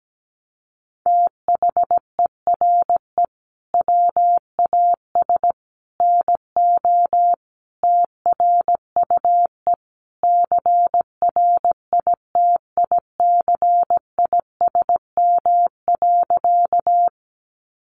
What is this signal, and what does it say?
Morse code, heard as a keyed tone, about 17 wpm, 700 Hz. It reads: THERE WAS NO TRUE CRITICISM.
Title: Sentences from Top 1300 Words 17wpm